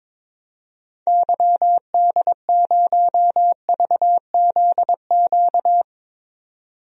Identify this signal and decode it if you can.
Morse code, heard as a keyed tone, about 22 wpm, 700 Hz. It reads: YD0VZQ
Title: DXCC-100 Call Signs - 6 Characters - 22wpm